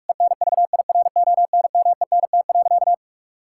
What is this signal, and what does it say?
Morse code, heard as a keyed tone, about 42 wpm, 700 Hz. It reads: ENVIRONMENT.
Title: Sentences from Top 1000 Words - Review Part 5 - 45wpm